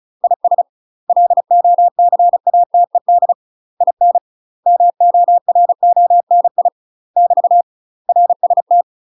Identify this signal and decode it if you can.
Morse code, heard as a keyed tone, about 36 wpm, 700 Hz. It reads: IS LOCATED IN MORONI <BT> RST
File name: UK MASTER QSOs TRAINING / 35WPM / UK MASTER QSOs 0020